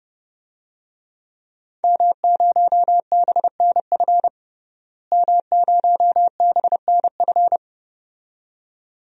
Morse code, heard as a keyed tone, about 30 wpm, 700 Hz. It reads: M0BNF M0BNF
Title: DXCC-100 Call Signs - 5 Characters - Repeated 2x - 30wpm